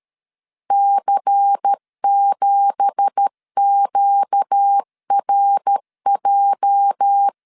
Morse code, heard as a keyed tone, about 13 wpm, 800 Hz. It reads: C7QRJ
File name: LEZIONI / 9 2867 12wpm 2322